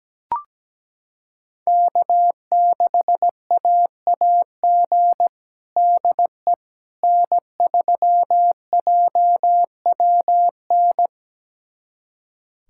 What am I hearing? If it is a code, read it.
Morse code, heard as a keyed tone, about 17 wpm, 700 Hz. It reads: K6AAG DE N3JWN
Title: USA Callsign DE Callsign - 5 Characters - 17wpm